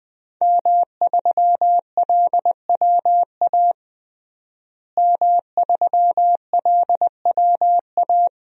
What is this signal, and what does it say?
Morse code, heard as a keyed tone, about 20 wpm, 700 Hz. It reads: M3LWA M3LWA
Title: DXCC-100 Call Signs - 5 Characters - Repeated 3x - 20wpm